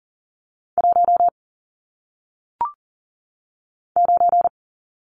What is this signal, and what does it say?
Morse code, heard as a keyed tone, about 41 wpm, 700 Hz. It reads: 1 9
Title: Numbers - 1-Digit 40wpm